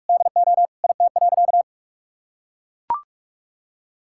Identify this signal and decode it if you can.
Morse code, heard as a keyed tone, about 41 wpm, 700 Hz. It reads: DO IT.
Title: Sentences from Top 1600 Words 45wpm